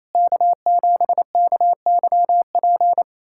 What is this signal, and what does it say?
Morse code, heard as a keyed tone, about 28 wpm, 700 Hz. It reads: K7KYP
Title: USA Callsign DE Callsign - 5 Characters - 28wpm